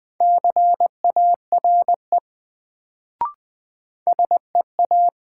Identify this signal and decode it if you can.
Morse code, heard as a keyed tone, about 20 wpm, 700 Hz. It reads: CARE SEA
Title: Sentences from Top 300 Words - New Word Drill - 20wpm